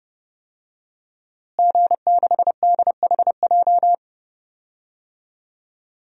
Morse code, heard as a keyed tone, about 32 wpm, 700 Hz. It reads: G6DHJ
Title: DXCC-100 Call Signs - 5 Characters - Repeated 3x - 30wpm